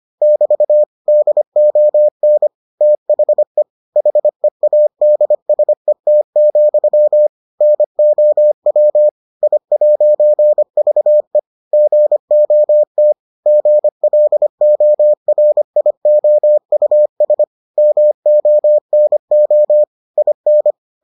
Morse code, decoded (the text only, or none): <BT> DON THE HEADSET, NOW I'VE GOT GLORIOUS MONO IN